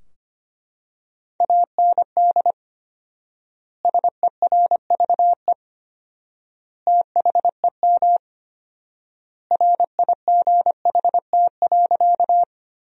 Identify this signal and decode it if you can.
Morse code, heard as a keyed tone, about 25 wpm, 700 Hz. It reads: AND SERVE THEM RIGHT.
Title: 4x Word-Spacing - Sentences from Top 400 Words - 25wpm